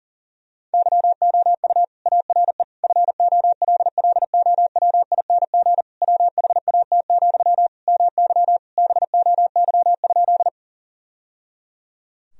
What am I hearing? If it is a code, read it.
Morse code, heard as a keyed tone, about 40 wpm, 690 Hz. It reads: YOU ARE FOLLOWING WHAT, MY BOY?
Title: Sentences from Top 1000 Words - Review Part 4 - 40wpm